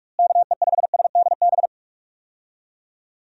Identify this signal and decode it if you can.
Morse code, heard as a keyed tone, about 46 wpm, 700 Hz. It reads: KE5SDB
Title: USA Call Signs - 6 Characters - Repeated 2x - 45wpm